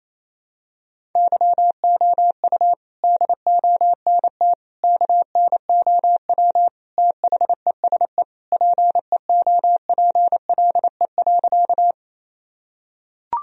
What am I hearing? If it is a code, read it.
Morse code, heard as a keyed tone, about 28 wpm, 700 Hz. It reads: YOU DONT KNOW THESE PEOPLE.
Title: Sentences from Top 1000 Words - Review Part 1 - 28wpm